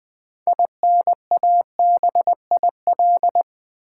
Morse code, heard as a keyed tone, about 20 wpm, 700 Hz. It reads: INABIL
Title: Build A Word - 3 Letters At A Time - 9-Letter Words - 20wpm